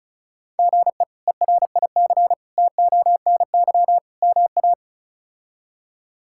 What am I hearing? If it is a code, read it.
Morse code, heard as a keyed tone, about 36 wpm, 700 Hz. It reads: GE ERIC TONY MA
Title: K1USN SST QSO Practice - 35wpm